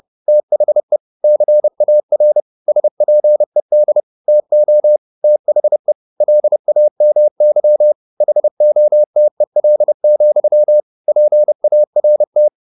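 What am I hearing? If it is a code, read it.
Morse code, heard as a keyed tone, about 30 wpm, 600 Hz. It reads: THE CAR SPED TO THE LAMY HOTEL, PART
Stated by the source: Morse code at 30 wpm, the novel Never by Ken Follett in English, from CW med Gnister.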